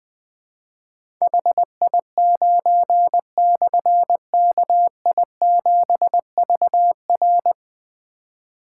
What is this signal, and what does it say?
Morse code, heard as a keyed tone, about 20 wpm, 700 Hz. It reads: HI9/KI7VR
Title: Difficult Call Signs 20wpm